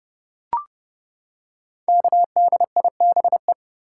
Morse code, heard as a keyed tone, about 30 wpm, 700 Hz. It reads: KDIBE
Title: Code Groups - 5 Letters - 30wpm